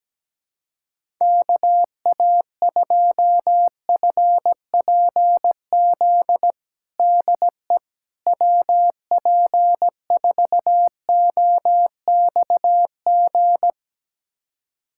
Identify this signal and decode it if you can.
Morse code, heard as a keyed tone, about 17 wpm, 700 Hz. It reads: KA2FPZ DE WP4OXG
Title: USA Callsign DE Callsign - 6 Characters - 17wpm